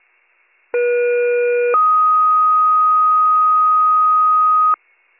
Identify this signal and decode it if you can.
Two-tone page, heard as a one-second tone, then a three-second tone, about 480 then 1250 Hz. It is heard from 0.7 to 4.7 s.